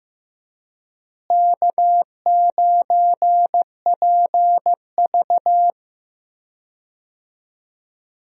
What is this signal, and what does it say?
Morse code, heard as a keyed tone, about 16 wpm, 700 Hz. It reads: K9PV
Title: USA Call Signs - 4 Characters - Repeated 2x - 15wpm